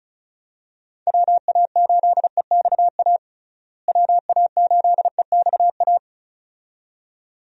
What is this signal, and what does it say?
Morse code, heard as a keyed tone, about 36 wpm, 700 Hz. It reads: WA8EXA WA8EXA
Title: USA Call Signs - 6 Characters - Repeated 2x - 35wpm